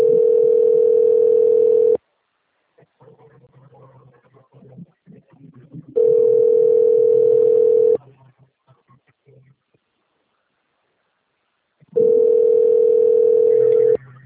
Ringback tone (North America), 440 + 480 Hz.